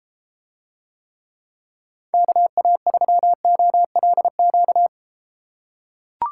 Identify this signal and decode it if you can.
Morse code, heard as a keyed tone, about 33 wpm, 700 Hz. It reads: KA3OLQ